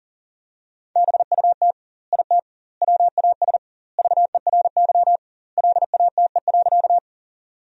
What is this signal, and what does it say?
Morse code, heard as a keyed tone, about 40 wpm, 700 Hz. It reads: BUT IT WAS VERY LATE.
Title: 2x Word-Spacing - Sentences from Top 300 Words - 40wpm